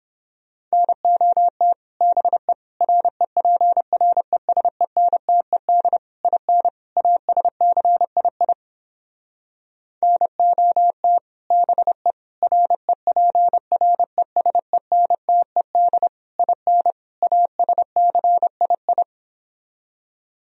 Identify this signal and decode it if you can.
Morse code, heard as a keyed tone, about 31 wpm, 700 Hz. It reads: NOT BE REPRESENTED IN ASCII NOT BE REPRESENTED IN ASCII